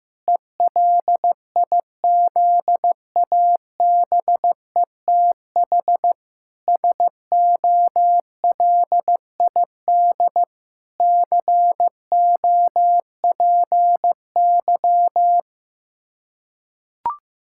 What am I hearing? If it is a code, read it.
Morse code, heard as a keyed tone, about 15 wpm, 700 Hz. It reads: ELIZABETH SOLID COPY
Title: Practice QSO - Exchange 3 - More Info - 15wpm